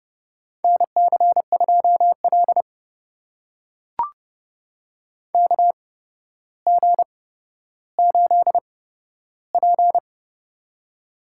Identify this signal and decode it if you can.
Morse code, heard as a keyed tone, about 30 wpm, spaced as for 13 wpm, 700 Hz. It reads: NC2L KG8P
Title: USA Call Signs - 4 Characters - 8x Character Spacing - 30wpm